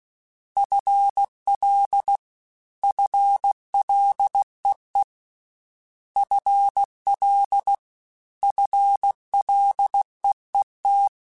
Morse code, heard as a keyed tone, about 16 wpm, 800 Hz. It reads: FL FLEE FL FLEET
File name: FL / FL mp3s / FL Prefixes 1x1 @15